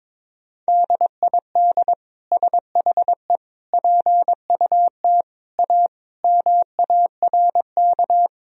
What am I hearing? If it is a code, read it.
Morse code, heard as a keyed tone, about 22 wpm, 700 Hz. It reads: DID SHE PUT A MARK
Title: Sentences from Top 1000 Words - Review Part 1 - 22wpm